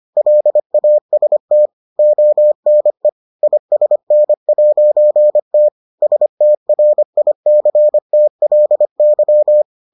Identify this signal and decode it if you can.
Morse code, heard as a keyed tone, about 25 wpm, 600 Hz. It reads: LAST ONE ISN'T STRICTLY